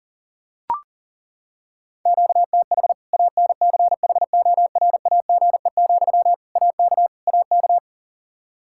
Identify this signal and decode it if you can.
Morse code, heard as a keyed tone, about 40 wpm, 700 Hz. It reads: QTH ANCHORAGE, AK AK
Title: Practice QSO - Exchange 1 - Beginning - 40wpm